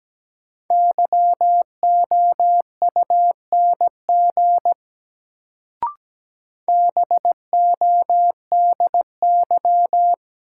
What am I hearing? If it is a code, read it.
Morse code, heard as a keyed tone, about 17 wpm, 700 Hz. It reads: YOUNG BODY
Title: Sentences from Top 400 Words - New Word Drill - 17wpm